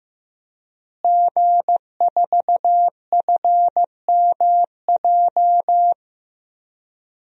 Morse code, heard as a keyed tone, about 15 wpm, 700 Hz. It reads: G4FMJ
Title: DXCC-100 Call Signs - 5 Characters - Repeated 3x - 15wpm